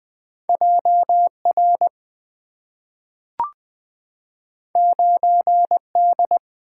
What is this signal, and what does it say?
Morse code, heard as a keyed tone, about 20 wpm, 700 Hz. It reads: JR 9D
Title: Two Letter-Number Combinations - 20wpm